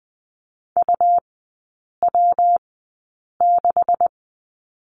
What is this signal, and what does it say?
Morse code, heard as a keyed tone, about 20 wpm, spaced as for 8 wpm, 700 Hz. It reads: UW6
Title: 127 Lesson 21e - Number 6 - Three Characters 20wpm